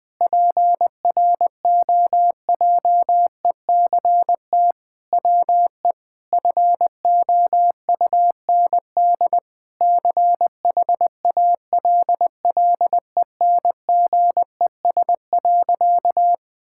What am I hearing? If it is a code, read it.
Morse code, heard as a keyed tone, about 20 wpm, 700 Hz. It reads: PROJECT WE FOUND CHALLENGES.
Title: Sentences with 6-Word Phrases - 20wpm